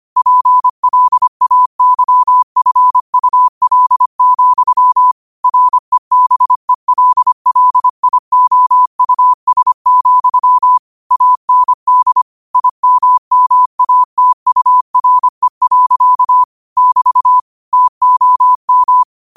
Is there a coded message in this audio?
Yes: Morse code from 0.2 to 19.0 s, heard as a keyed tone, about 25 wpm, 1000 Hz. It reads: PLAYFUL, REBELLIOUS, AND IMMATURE. <BT> TOM